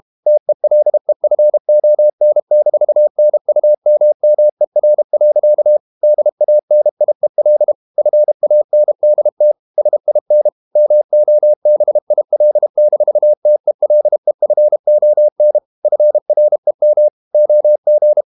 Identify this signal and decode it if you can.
Morse code, heard as a keyed tone, about 32 wpm, 600 Hz. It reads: TELEFON-NUMMER. DANIEL FANDT SIN MOBIL-TELEFON FREM OG
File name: Ferie QRQ 31wpm